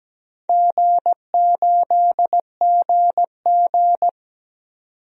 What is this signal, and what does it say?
Morse code, heard as a keyed tone, about 17 wpm, 700 Hz. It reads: G8GG
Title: DXCC-100 Call Signs - 4 Characters - 17wpm